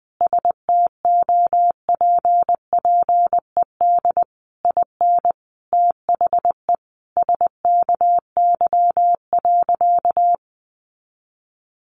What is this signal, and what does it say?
Morse code, heard as a keyed tone, about 20 wpm, 700 Hz. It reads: STOPPED IN THE SKY.
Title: A Sound Of Thunder 20wpm